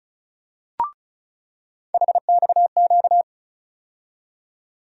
Morse code, heard as a keyed tone, about 36 wpm, 700 Hz. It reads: HXQ
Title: USA Call Sign Postfixes - 3 Letters 35wpm